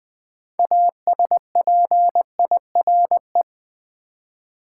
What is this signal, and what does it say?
Morse code, heard as a keyed tone, about 20 wpm, 700 Hz. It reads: ASPIRE
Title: 6 Letter Words - First 3 Letters Given - 20wpm